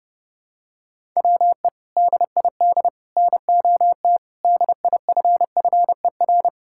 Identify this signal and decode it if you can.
Morse code, heard as a keyed tone, about 31 wpm, 700 Hz. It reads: WE DID NOT DIFFER
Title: Sentences from Top 1000 Words - Review Part 6 - 30wpm